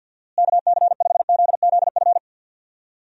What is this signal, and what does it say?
Morse code, heard as a keyed tone, about 52 wpm, 700 Hz. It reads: KC5ZZF
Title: USA Call Signs - 6 Characters - Repeated 3x - 50wpm